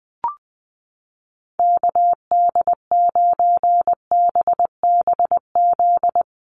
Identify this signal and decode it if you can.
Morse code, heard as a keyed tone, about 20 wpm, 700 Hz. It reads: KD9BBZ
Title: USA Call Signs Repeated 3x 20wpm